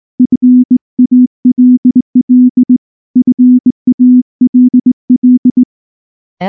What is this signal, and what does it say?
Morse code, heard as a keyed tone, about 18 wpm, 250 Hz. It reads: FALL FALL
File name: morse / morse-training-20WPM-4